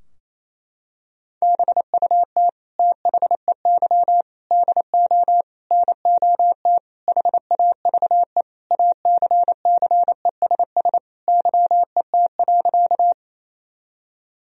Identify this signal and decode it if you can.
Morse code, heard as a keyed tone, about 28 wpm, 700 Hz. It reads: BUT THEY DO NOT HAVE ACCESS YET.
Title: Sentences from Top 1000 Words - Review Part 4 - 28wpm